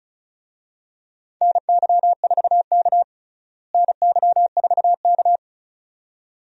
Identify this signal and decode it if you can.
Morse code, heard as a keyed tone, about 34 wpm, 700 Hz. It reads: NY4K NY4K
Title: USA Call Signs - 4 Characters - Repeated 2x - 35wpm